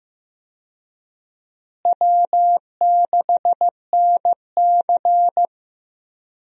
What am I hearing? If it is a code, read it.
Morse code, heard as a keyed tone, about 15 wpm, 700 Hz. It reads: W6NC
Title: USA Call Signs - 4 Characters - Repeated 3x - 15wpm